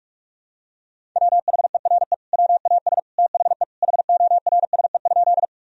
Morse code, heard as a keyed tone, about 46 wpm, 700 Hz. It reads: WHERE WAS THE HORSE?
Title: Sentences from Top 1000 Words - Review Part 1 - 45wpm